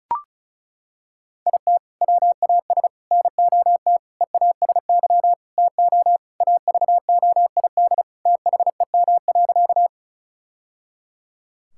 Morse code, heard as a keyed tone, about 34 wpm, 700 Hz. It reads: IT WAS NOT EASY TO AVOID THEM.
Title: Sentences from Top 1000 Words - Review Part 6 - 35wpm